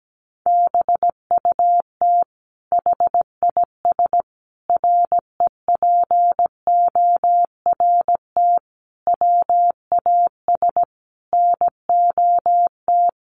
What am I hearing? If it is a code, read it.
Morse code, heard as a keyed tone, about 17 wpm, 700 Hz. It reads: BUT HIS REPORT WAS NOT
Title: Sentences from Top 1200 Words 17wpm